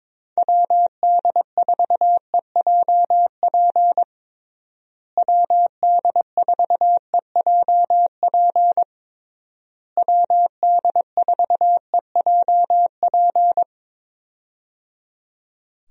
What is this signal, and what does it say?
Morse code, heard as a keyed tone, about 22 wpm, 700 Hz. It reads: WD4EJP WD4EJP WD4EJP
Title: USA Call Signs - 6 Characters - Repeated 3x - 22wpm